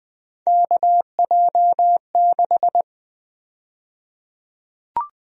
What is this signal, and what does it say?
Morse code, heard as a keyed tone, about 20 wpm, 700 Hz. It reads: KJ6
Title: USA Call Sign Prefixes - 20wpm